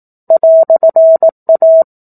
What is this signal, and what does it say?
Morse code, heard as a keyed tone, about 18 wpm, 650 Hz. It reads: "A
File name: 04-Chapter-04 18 WPM